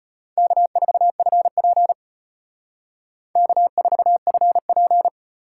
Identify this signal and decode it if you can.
Morse code, heard as a keyed tone, about 32 wpm, 700 Hz. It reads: K4FP K4FP